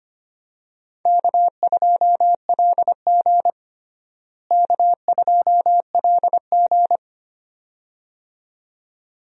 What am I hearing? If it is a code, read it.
Morse code, heard as a keyed tone, about 25 wpm, 700 Hz. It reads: K2LG K2LG
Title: USA Call Signs - 4 Characters - Repeated 2x - 25wpm